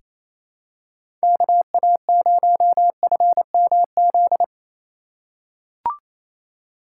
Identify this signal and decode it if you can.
Morse code, heard as a keyed tone, about 28 wpm, 700 Hz. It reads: KA0FMZ
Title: USA Call Signs - 6 Characters - 2x Character Spacing - 28wpm